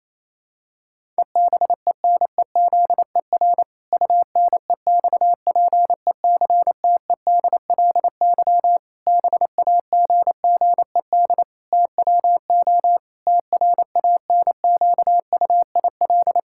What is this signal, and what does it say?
Morse code, heard as a keyed tone, about 27 wpm, 700 Hz. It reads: EBENEZER UNEXPECTEDLY BAGGED TWO TRANQUIL
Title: Pangram Challenge v2 - 28wpm